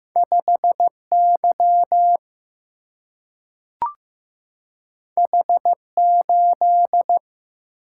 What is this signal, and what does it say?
Morse code, heard as a keyed tone, about 15 wpm, 700 Hz. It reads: 5Y H8
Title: Single-Letter Single-Number 15wpm